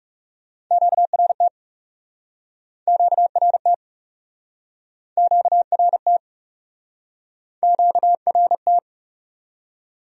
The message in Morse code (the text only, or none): QRT QRT QRT QRT